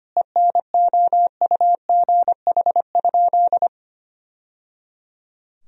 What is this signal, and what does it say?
Morse code, heard as a keyed tone, about 25 wpm, 700 Hz. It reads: ENOUGH?
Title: Sentences from Top 1000 Words - Review Part 4 - 25wpm